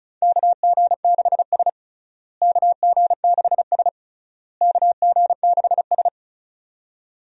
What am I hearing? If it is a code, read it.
Morse code, heard as a keyed tone, about 36 wpm, 700 Hz. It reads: KG6S KG6S KG6S
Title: USA Call Signs - 4 Characters - Repeated 3x - 35wpm